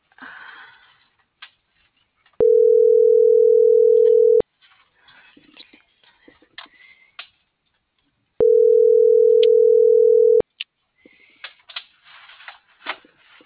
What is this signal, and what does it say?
Ringback tone (North America), 440 + 480 Hz.